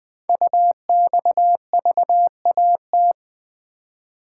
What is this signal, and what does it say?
Morse code, heard as a keyed tone, about 20 wpm, 700 Hz. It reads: UXVAT
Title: Code Groups - 5 Letters - 20wpm